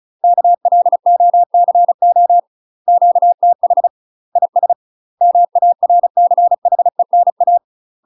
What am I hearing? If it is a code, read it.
Morse code, heard as a keyed tone, about 36 wpm, 700 Hz. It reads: KLOCO QTH IS MARCHENA
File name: UK MASTER QSOs TRAINING / 35WPM / UK MASTER QSOs 0036